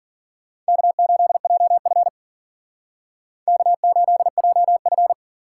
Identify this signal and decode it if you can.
Morse code, heard as a keyed tone, about 51 wpm, 700 Hz. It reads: K8JF K8JF